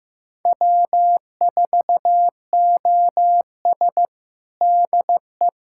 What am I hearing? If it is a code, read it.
Morse code, heard as a keyed tone, about 15 wpm, 700 Hz. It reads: W4OS DE
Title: USA Callsign DE Callsign - 4 Characters - 15wpm